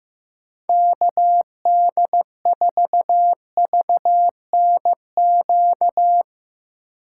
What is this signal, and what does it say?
Morse code, heard as a keyed tone, about 15 wpm, 700 Hz. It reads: KD4VNQ
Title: USA Call Signs - 6 Characters - 4x Character Spacing - 15wpm